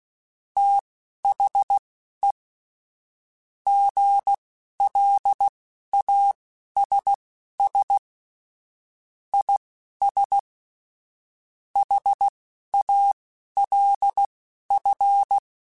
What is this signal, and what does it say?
Morse code, heard as a keyed tone, about 16 wpm, spaced as for 11 wpm, 800 Hz. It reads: THE GLASS IS HALF
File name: Chuckles / Chuckles-mp3s / Glass Half Full @10